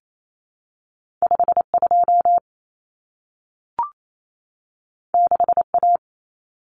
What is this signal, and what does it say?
Morse code, heard as a keyed tone, about 28 wpm, 700 Hz. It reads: 52 6A
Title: Common Cut Numbers - 2-Digit 28wpm